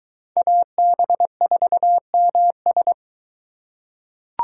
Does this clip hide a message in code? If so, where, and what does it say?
Yes: Morse code from 0.4 to 2.9 s, heard as a keyed tone, about 23 wpm, 700 Hz. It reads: AB4MS